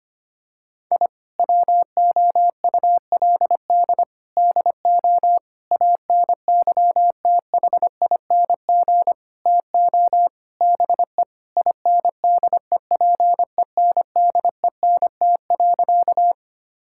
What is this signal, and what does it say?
Morse code, heard as a keyed tone, about 25 wpm, 700 Hz. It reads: I WOULD DO ANYTHING TO BE INDEPENDENT.
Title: Sentences from Top 1000 Words - Review Part 6 - 25wpm